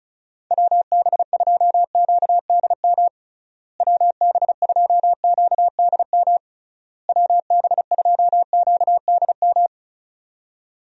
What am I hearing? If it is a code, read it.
Morse code, heard as a keyed tone, about 34 wpm, 700 Hz. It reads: WB2QDM WB2QDM WB2QDM
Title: USA Call Signs - 6 Characters - Repeated 3x - 35wpm